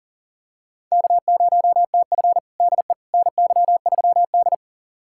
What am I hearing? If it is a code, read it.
Morse code, heard as a keyed tone, about 37 wpm, 700 Hz. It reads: K0TF DE NY3D
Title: USA Callsign DE Callsign - 4 Characters - 40wpm